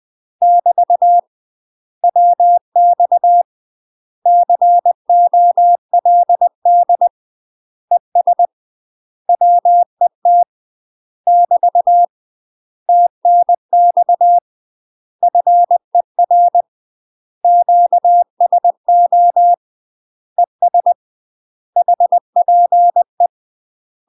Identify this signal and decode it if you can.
Morse code, heard as a keyed tone, about 20 wpm, 700 Hz. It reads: <BT> WX COLD ES WET <BT> TNX FER QSO ES HPE
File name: UK MASTER QSOs TRAINING / 20WPM / UK MASTER QSOs 0042